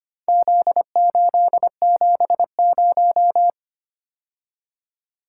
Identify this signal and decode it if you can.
Morse code, heard as a keyed tone, about 25 wpm, 700 Hz. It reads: Z870
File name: practice / 25wpm / Four Letter-Number Combinations - 25wpm